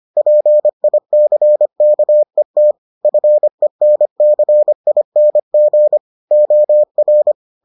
Morse code, heard as a keyed tone, about 25 wpm, 600 Hz. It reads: PICKET FENCING OR